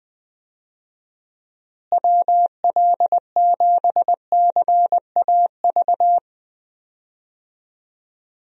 Morse code, heard as a keyed tone, about 20 wpm, 700 Hz. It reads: WL7CAV
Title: USA Call Signs - 6 Characters - Repeated 3x - 20wpm